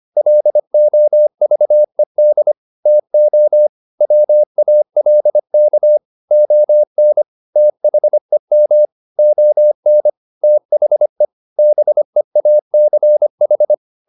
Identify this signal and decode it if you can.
Morse code, heard as a keyed tone, about 25 wpm, 600 Hz. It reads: LOVED TO WALK ON THEM ON THE BEACH
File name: the old man and the sea part 1 25wpm